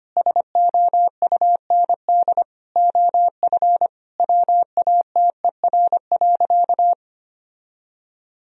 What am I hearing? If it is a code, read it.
Morse code, heard as a keyed tone, about 25 wpm, 700 Hz. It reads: SOUND OF WATER.
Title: Sentences from Top 1000 Words - Review Part 1 - 25wpm